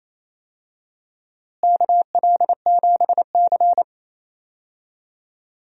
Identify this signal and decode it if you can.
Morse code, heard as a keyed tone, about 28 wpm, 700 Hz. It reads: KL7C
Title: DXCC-100 Call Signs - 4 Characters - Repeated 2x - 28wpm